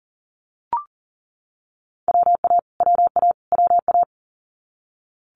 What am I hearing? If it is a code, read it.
Morse code, heard as a keyed tone, about 41 wpm, 700 Hz. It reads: WA WA WA
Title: Multi-letter Phonemes Repeated 3x 40wpm